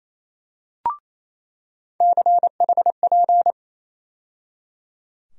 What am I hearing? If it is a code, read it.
Morse code, heard as a keyed tone, about 28 wpm, 700 Hz. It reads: CHP
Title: USA Call Sign Postfixes - 3 Letters 28wpm